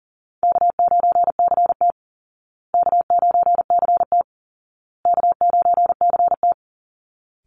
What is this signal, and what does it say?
Morse code, heard as a keyed tone, about 40 wpm, 700 Hz. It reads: K9CT K9CT K9CT
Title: Easy Call Signs Repeated 3x - 40wpm